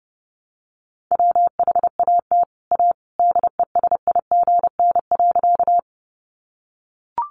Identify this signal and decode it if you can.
Morse code, heard as a keyed tone, about 30 wpm, 700 Hz. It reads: WHAT A DESIGN.
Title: Sentences from Top 800 Words 30wpm